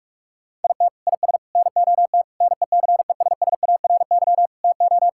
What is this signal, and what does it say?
Morse code, heard as a keyed tone, about 46 wpm, 700 Hz. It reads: IT IS NOT NECESSARY TO
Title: Sentences from Top 1000 Words - Review Part 6 - 45wpm